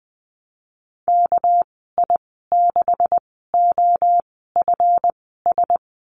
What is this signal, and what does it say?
Morse code, heard as a keyed tone, about 20 wpm, spaced as for 15 wpm, 700 Hz. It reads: KI6OFS
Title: 245 Lesson 39g - Letter z - Call Signs 20wpm